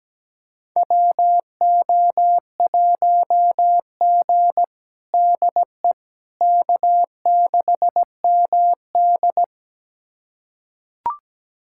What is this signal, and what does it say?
Morse code, heard as a keyed tone, about 17 wpm, 700 Hz. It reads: WO1G DE K6MD
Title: USA Callsign DE Callsign - 4 Characters - 17wpm